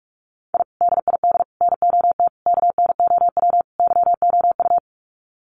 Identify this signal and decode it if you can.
Morse code, heard as a keyed tone, about 44 wpm, 700 Hz. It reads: I DID NOT KNOW YOU
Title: Sets of 5 Words Encore2 45wpm